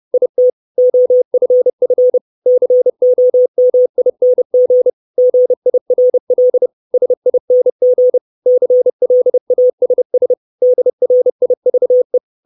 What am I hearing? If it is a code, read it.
Morse code, heard as a keyed tone, about 30 wpm, 500 Hz. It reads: IT OFF COMING GIRL SING CLASS DRIVE
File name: 632 Words 30WPM